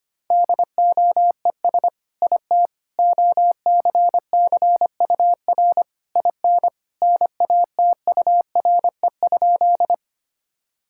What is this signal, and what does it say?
Morse code, heard as a keyed tone, about 25 wpm, 700 Hz. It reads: DOES IT OCCUR IN NATURE?
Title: Sentences from Top 1000 Words - Review Part 6 - 25wpm